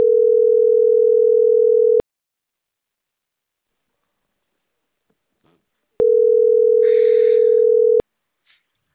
Ringback tone (North America), 440 + 480 Hz.